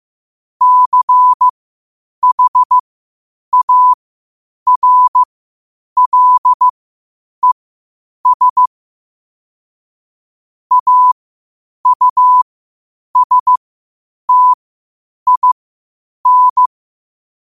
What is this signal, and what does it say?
Morse code, heard as a keyed tone, about 15 wpm, spaced as for 8 wpm, 1000 Hz. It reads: CHARLES AUSTIN
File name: qotdfiles / 8wpm